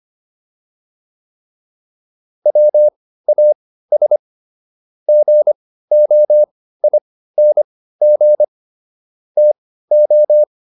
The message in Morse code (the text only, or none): WAS GOING TO